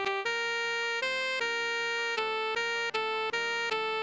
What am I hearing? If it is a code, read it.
There is no code.